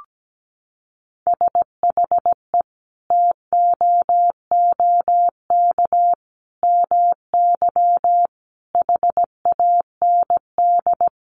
Morse code, heard as a keyed tone, about 17 wpm, 700 Hz. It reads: SHE TOOK MY HAND
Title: Sentences from Top 1500 Words 17wpm